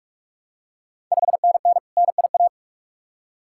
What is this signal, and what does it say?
Morse code, heard as a keyed tone, about 46 wpm, 700 Hz. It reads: 5NN NIA
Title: Ontario QSO Party Exchange 45wpm